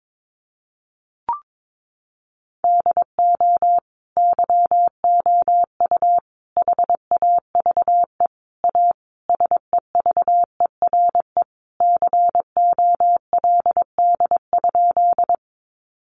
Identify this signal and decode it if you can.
Morse code, heard as a keyed tone, about 22 wpm, 700 Hz. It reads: DO YOU HAVE A SEVERE COLD?
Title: Sentences from Top 1400 Words 22wpm